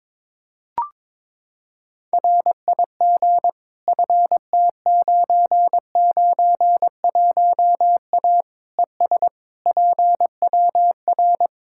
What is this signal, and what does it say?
Morse code, heard as a keyed tone, about 22 wpm, 700 Hz. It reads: RIG FT991A ES PWR
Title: Practice QSO - Exchange 2 - Info - 22wpm